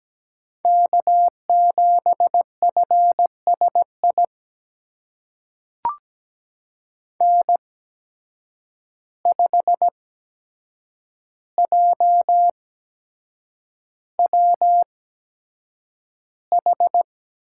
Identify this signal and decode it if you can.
Morse code, heard as a keyed tone, about 17 wpm, spaced as for 7 wpm, 700 Hz. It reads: K7FSI N5JWH
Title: USA Call Signs - 5 Characters - 8x Character Spacing - 17wpm